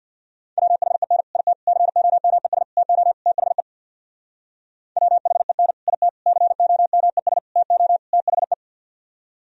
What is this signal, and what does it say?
Morse code, heard as a keyed tone, about 65 wpm, 700 Hz. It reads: WHEN IT COMES TO THE WHEN IT COMES TO THE